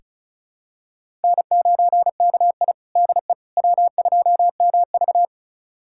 Morse code, heard as a keyed tone, about 34 wpm, 700 Hz. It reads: N9KI DE W2MV
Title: USA Callsign DE Callsign - 4 Characters - 35wpm